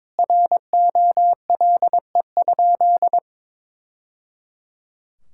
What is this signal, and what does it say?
Morse code, heard as a keyed tone, about 22 wpm, 700 Hz. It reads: ROLE?
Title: Sentences from Top 1000 Words - Review Part 6 - 22wpm